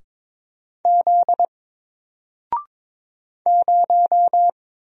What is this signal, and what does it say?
Morse code, heard as a keyed tone, about 22 wpm, 700 Hz. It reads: Z 0
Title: Single Letter-Number Alt 22wpm